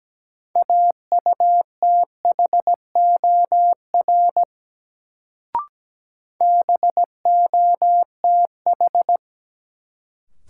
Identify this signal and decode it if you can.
Morse code, heard as a keyed tone, about 17 wpm, 700 Hz. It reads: AUTHOR BOTH
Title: Sets of 4 Words - New Word Drill - 17wpm